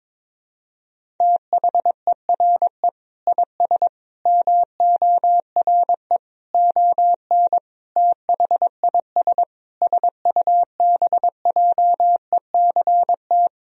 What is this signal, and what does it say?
Morse code, heard as a keyed tone, about 22 wpm, 700 Hz. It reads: THERE IS MORE ON THIS SUBJECT
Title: Sentences from Top 1000 Words - Review Part 1 - 22wpm